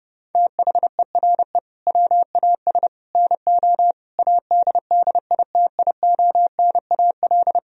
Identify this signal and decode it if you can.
Morse code, heard as a keyed tone, about 32 wpm, 700 Hz. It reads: THERE WAS NO ADDITIONAL
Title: Sentences from Top 1000 Words - Review Part 4 - 30wpm